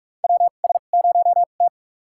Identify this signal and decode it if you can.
Morse code, heard as a keyed tone, about 43 wpm, spaced as for 32 wpm, 700 Hz. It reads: WS0T
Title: USA Call Signs - 4 Characters - 2x Character Spacing - 45wpm